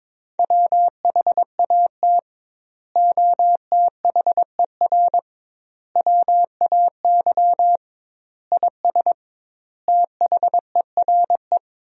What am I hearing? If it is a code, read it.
Morse code, heard as a keyed tone, about 22 wpm, 700 Hz. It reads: WHAT OTHER WAY IS THERE
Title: 2x Word-Spacing - Sentences from Top 100 Words - 22wpm